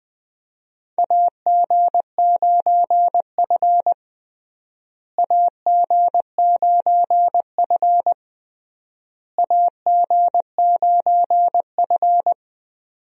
Morse code, heard as a keyed tone, about 20 wpm, 700 Hz. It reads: AG9F AG9F AG9F
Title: USA Call Signs - 4 Characters - Repeated 3x - 20wpm